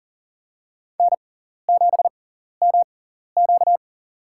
Morse code, heard as a keyed tone, about 40 wpm, spaced as for 14 wpm, 700 Hz. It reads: N7MQ
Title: USA Call Signs - 4 Characters - 6x Character Spacing - 40wpm